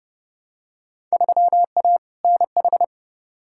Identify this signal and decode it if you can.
Morse code, heard as a keyed tone, about 30 wpm, 700 Hz. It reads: 3A NH
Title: Field Day Exchanges Encore - 30wpm